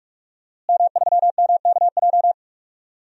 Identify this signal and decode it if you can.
Morse code, heard as a keyed tone, about 46 wpm, 700 Hz. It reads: M3MKJ
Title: DXCC-100 Call Signs - 5 Characters - Repeated 3x - 45wpm